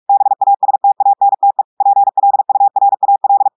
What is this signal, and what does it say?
Morse code, heard as a keyed tone, about 46 wpm, 800 Hz. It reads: BASTANTE PLURAL
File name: audios / portugais / portugais texte en clair 1 vitesse 45 mots minutes